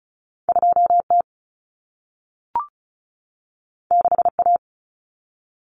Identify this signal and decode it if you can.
Morse code, heard as a keyed tone, about 36 wpm, 700 Hz. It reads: 2T 6A
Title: Common Cut Numbers - 2-Digit 35wpm